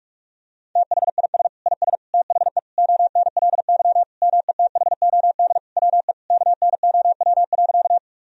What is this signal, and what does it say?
Morse code, heard as a keyed tone, about 41 wpm, 700 Hz. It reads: THIS IS THE ONLY METHOD WE KNOW.
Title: Sentences from Top 1000 Words - Review Part 3 - 45wpm